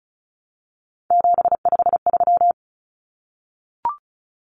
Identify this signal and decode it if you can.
Morse code, heard as a keyed tone, about 34 wpm, 700 Hz. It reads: 753
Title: Common Cut Numbers - 3-Digit 35wpm